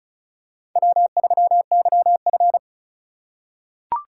Morse code, heard as a keyed tone, about 36 wpm, 690 Hz. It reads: W3YF
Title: USA Call Signs - 4 Characters - 2x Character Spacing - 35wpm